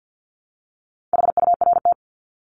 Morse code, heard as a keyed tone, about 50 wpm, 700 Hz. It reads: HURT
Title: Top 1751-2000 Words 50wpm